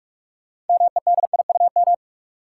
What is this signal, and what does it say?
Morse code, heard as a keyed tone, about 45 wpm, 700 Hz. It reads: MEDIUM
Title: Sentences from Top 900 Words - New Word Drill - 45wpm